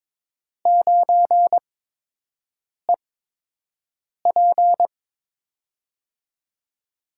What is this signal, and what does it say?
Morse code, heard as a keyed tone, about 22 wpm, spaced as for 6 wpm, 700 Hz. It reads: 9EP
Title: USA Call Signs - 5 Characters - 8x Character Spacing - 22wpm